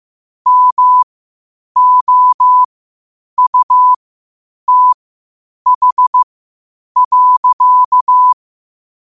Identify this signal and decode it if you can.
Morse code, heard as a keyed tone, about 15 wpm, spaced as for 8 wpm, 1000 Hz. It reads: MOUTH.
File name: qotdfiles / 8wpm